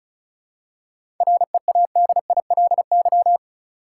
Morse code, heard as a keyed tone, about 36 wpm, 700 Hz. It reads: READILY
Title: Seven Letter Words - 35wpm